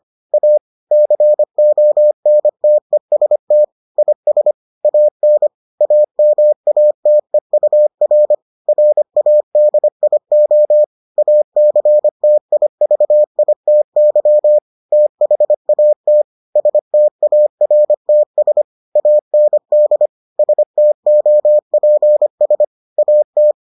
Morse code, heard as a keyed tone, about 25 wpm, 600 Hz. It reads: A CONTEST IS AN AMATEUR RADIO ACTIVITY THAT STARTS AND STOPS AT